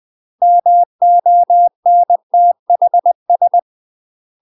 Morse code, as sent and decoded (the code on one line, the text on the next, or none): -- --- -. - .... ...
MONTHS